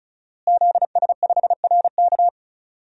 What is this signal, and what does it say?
Morse code, heard as a keyed tone, about 36 wpm, 700 Hz. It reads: ZS5RK
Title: DXCC-100 Call Signs - 5 Characters - Repeated 2x - 35wpm